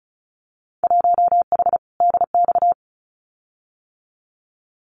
Morse code, heard as a keyed tone, about 36 wpm, 700 Hz. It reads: 1H DX
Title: Winter Field Day Exchanges 35wpm